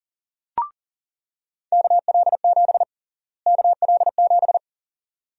Morse code, heard as a keyed tone, about 37 wpm, 700 Hz. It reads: KL7 KL7
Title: USA Call Sign Prefixes - Repeated 2x - 40wpm